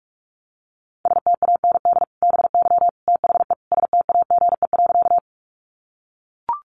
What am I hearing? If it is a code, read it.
Morse code, heard as a keyed tone, about 44 wpm, 700 Hz. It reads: STAND BY THE STAGE.
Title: Sentences from Top 1200 Words 45wpm